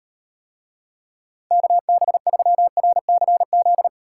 Morse code, heard as a keyed tone, about 38 wpm, 700 Hz. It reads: KB3RCZ